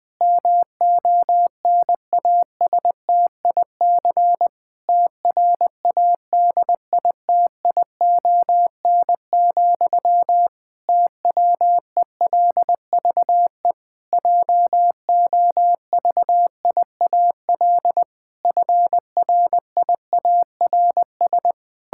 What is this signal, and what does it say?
Morse code, heard as a keyed tone, about 20 wpm, 700 Hz. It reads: MONASTIC TRADITION, TWELVE JOVIAL FRIARS
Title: Pangram Challenge v2 - 20wpm